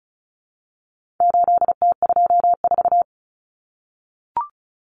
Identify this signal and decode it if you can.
Morse code, heard as a keyed tone, about 34 wpm, 700 Hz. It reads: 8T24
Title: Common Cut Numbers - 4-Digit 35wpm